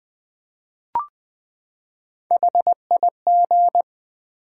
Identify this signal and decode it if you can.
Morse code, heard as a keyed tone, about 20 wpm, 700 Hz. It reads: HIG
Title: Build A Word - 3 Letters At A Time - 9-Letter Words - 20wpm